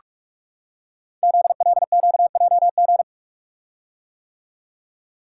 Morse code, heard as a keyed tone, about 44 wpm, 690 Hz. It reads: ZLQJG
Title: Code Groups - 5 Letters - 45wpm